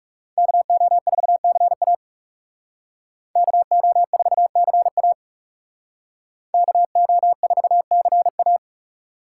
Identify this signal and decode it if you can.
Morse code, heard as a keyed tone, about 38 wpm, 700 Hz. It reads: KO4CA KO4CA KO4CA